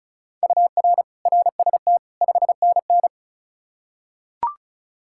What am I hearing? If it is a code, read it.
Morse code, heard as a keyed tone, about 36 wpm, 700 Hz. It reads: UR RST 5NN
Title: K1USN SST QSO Practice 35wpm